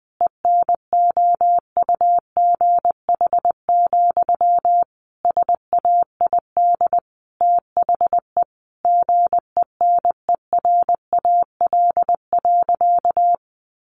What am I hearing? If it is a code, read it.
Morse code, heard as a keyed tone, about 20 wpm, 700 Hz. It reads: ENOUGH, SAID THE GENERAL.
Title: Sentences from Top 600 Words 20wpm